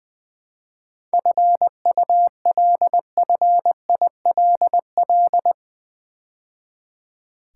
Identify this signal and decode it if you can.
Morse code, heard as a keyed tone, about 20 wpm, 700 Hz. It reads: FULFILL
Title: 7 Letter Words - First 3 Letters Given - 20wpm